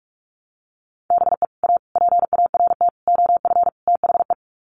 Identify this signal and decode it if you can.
Morse code, heard as a keyed tone, about 44 wpm, 700 Hz. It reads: BE A PART OF THE
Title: Sets of 5 Words Encore 45wpm